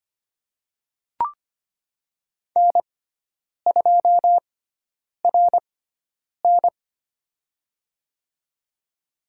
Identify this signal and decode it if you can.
Morse code, heard as a keyed tone, about 25 wpm, spaced as for 9 wpm, 700 Hz. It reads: N2RN
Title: USA Call Signs - 6x Character Spacing - 25wpm